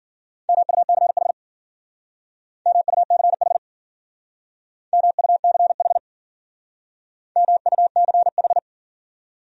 Morse code, heard as a keyed tone, about 50 wpm, 700 Hz. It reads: MUCH MUCH MUCH MUCH
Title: Speed-Racing Top 200 Words 40wpm